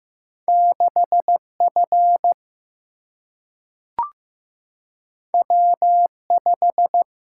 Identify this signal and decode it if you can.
Morse code, heard as a keyed tone, about 15 wpm, 700 Hz. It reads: 6F W5
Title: Single-Letter Single-Number 15wpm